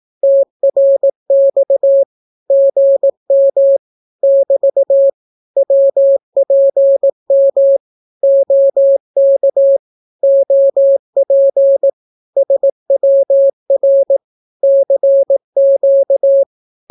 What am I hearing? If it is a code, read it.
Morse code, heard as a keyed tone, about 18 wpm, 550 Hz. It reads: TRX GM <BT> WPM OK OP SWR CQ
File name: naukapodstaw / skroty / skroty